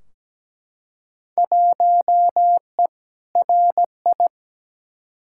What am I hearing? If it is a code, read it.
Morse code, heard as a keyed tone, about 17 wpm, 700 Hz. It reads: 1E RI
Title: Field Day Exchanges Encore - 17wpm